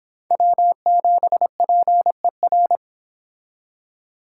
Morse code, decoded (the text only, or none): W7PER